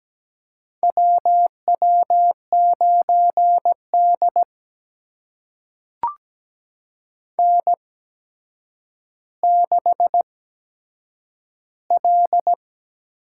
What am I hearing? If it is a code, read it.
Morse code, heard as a keyed tone, about 17 wpm, spaced as for 8 wpm, 700 Hz. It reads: WW9D N6L
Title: USA Call Signs - 4 Characters - 8x Character Spacing - 17wpm